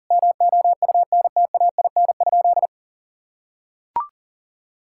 Morse code, heard as a keyed tone, about 40 wpm, 700 Hz. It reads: MOUNTAIN?
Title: Sentences from Top 1000 Words - Review Part 5 - 40wpm